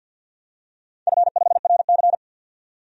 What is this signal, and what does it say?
Morse code, heard as a keyed tone, about 51 wpm, 700 Hz. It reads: F5RC
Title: DXCC-100 Call Signs - 4 Characters - 50wpm